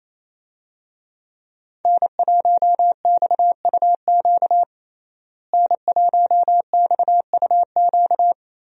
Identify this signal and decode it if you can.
Morse code, heard as a keyed tone, about 28 wpm, 700 Hz. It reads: N1XUQ N1XUQ
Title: USA Call Signs - 5 Characters - Repeated 3x - 28wpm